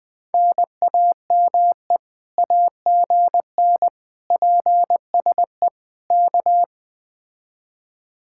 Morse code, heard as a keyed tone, about 20 wpm, 700 Hz. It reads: NAME AGN PSE K
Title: Speed-Racing Common QSO Phrases - 17wpm